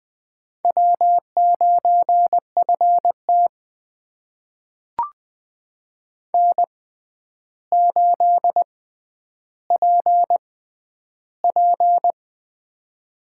Morse code, heard as a keyed tone, about 20 wpm, spaced as for 10 wpm, 700 Hz. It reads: W9FT N8PP
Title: USA Call Signs - 4 Characters - 6x Character Spacing - 20wpm